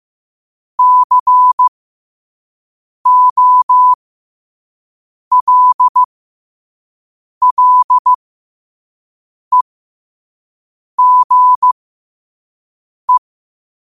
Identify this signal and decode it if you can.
Morse code, heard as a keyed tone, about 15 wpm, spaced as for 5 wpm, 1000 Hz. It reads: COLLEGE